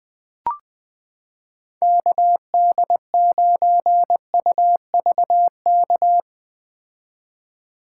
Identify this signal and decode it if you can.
Morse code, heard as a keyed tone, about 20 wpm, 700 Hz. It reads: KD9UVK
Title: 2023 POTA Hunter Call Signs - 20wpm